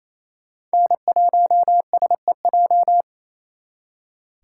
Morse code, heard as a keyed tone, about 35 wpm, 700 Hz. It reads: N1SEJ